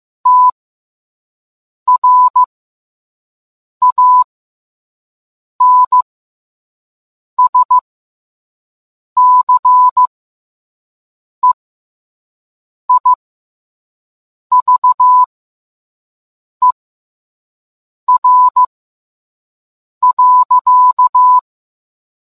Morse code, heard as a keyed tone, about 15 wpm, spaced as for 5 wpm, 1000 Hz. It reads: TRANSCEIVER.